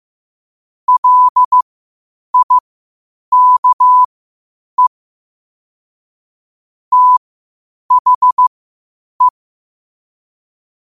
Morse code, heard as a keyed tone, about 15 wpm, spaced as for 8 wpm, 1000 Hz. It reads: LIKE THE
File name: qotdfiles / 8wpm